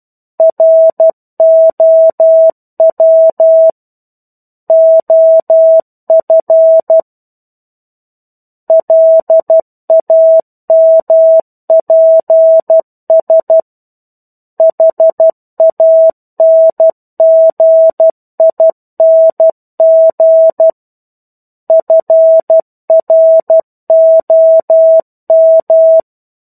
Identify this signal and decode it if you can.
Morse code, heard as a keyed tone, about 12 wpm, 650 Hz. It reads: ROW OF LAMPS HANGING FROM